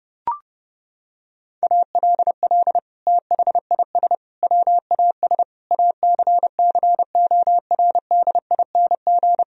Morse code, heard as a keyed tone, about 30 wpm, 700 Hz. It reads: ALL THIS WAS ACCORDING
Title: Sentences from Top 1000 Words - Review Part 4 - 30wpm